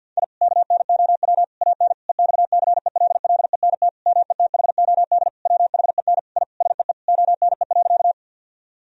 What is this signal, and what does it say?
Morse code, heard as a keyed tone, about 48 wpm, 700 Hz. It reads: I KNOW AN EXCELLENT METHOD WHEN I SEE ONE.
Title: Sentences from Top 1000 Words - Review Part 3 - 50wpm